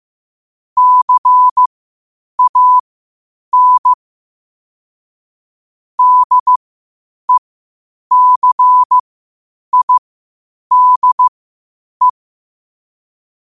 Morse code, heard as a keyed tone, about 15 wpm, spaced as for 8 wpm, 1000 Hz. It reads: CAN DECIDE
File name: qotdfiles / 8wpm